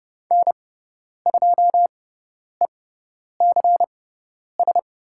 Morse code, heard as a keyed tone, about 30 wpm, spaced as for 10 wpm, 700 Hz. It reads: N2ECS
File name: Lesson 18g - Letter m - Call Signs 30wpm